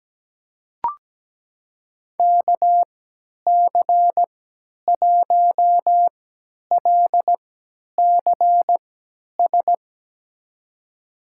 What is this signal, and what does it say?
Morse code, heard as a keyed tone, about 17 wpm, spaced as for 10 wpm, 700 Hz. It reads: KC1LCS
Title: USA Call Signs - 6 Characters - 3x Character Spacing - 17wpm